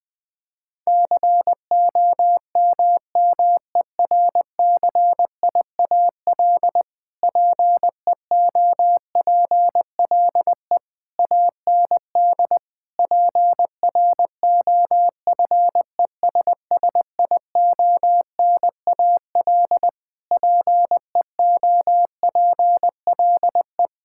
Morse code, heard as a keyed tone, about 20 wpm, 700 Hz. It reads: COMMERCIAL PEOPLE AND PROFESSIONAL PEOPLE
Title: Sentences from Top 1000 Words - Review Part 5 - 20wpm